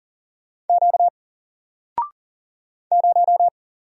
Morse code, heard as a keyed tone, about 40 wpm, 700 Hz. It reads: Q 0
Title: Single Letter-Number 40wpm